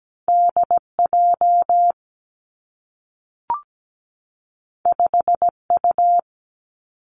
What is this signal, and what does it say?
Morse code, heard as a keyed tone, about 17 wpm, 700 Hz. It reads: DJ 5U
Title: Two Challenging Letter-Number Combinations 17wpm